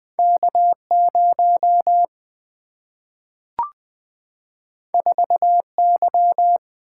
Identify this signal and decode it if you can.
Morse code, heard as a keyed tone, about 20 wpm, 700 Hz. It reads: K0 4Y
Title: Single-Letter Single-Number 20wpm